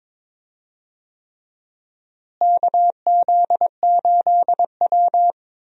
Morse code, heard as a keyed tone, about 22 wpm, 700 Hz. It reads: KZ8W
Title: USA Call Signs - 4 Characters - Repeated 3x - 22wpm